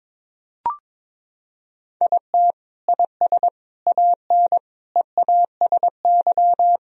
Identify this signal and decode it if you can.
Morse code, heard as a keyed tone, about 22 wpm, 700 Hz. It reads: IT IS AN EASY
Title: Sentences from Top 1000 Words - Review Part 2 - 22wpm